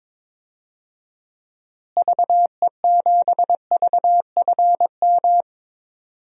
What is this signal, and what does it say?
Morse code, heard as a keyed tone, about 22 wpm, 700 Hz. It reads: VE7VFM
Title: DXCC-100 Call Signs - 6 Characters - Repeated 3x - 22wpm